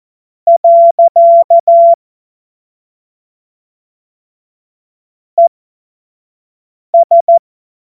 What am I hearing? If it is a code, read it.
Morse code, heard as a keyed tone, about 14 wpm, spaced as for 5 wpm, 680 Hz. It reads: . ES